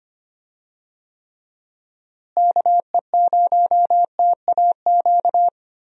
Morse code, heard as a keyed tone, about 25 wpm, 700 Hz. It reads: KE0TAQ
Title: USA Call Signs - 6 Characters - Repeated 2x - 25wpm